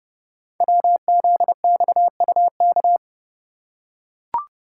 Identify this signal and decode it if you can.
Morse code, heard as a keyed tone, about 29 wpm, 700 Hz. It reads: WZXUK
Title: Code Groups - 5 Letter:Numbers - 30wpm